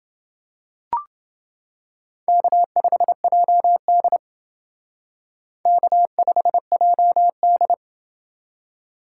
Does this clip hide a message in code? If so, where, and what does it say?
Yes: Morse code from 2.3 to 7.7 s, heard as a keyed tone, about 28 wpm, 700 Hz. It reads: K5JD K5JD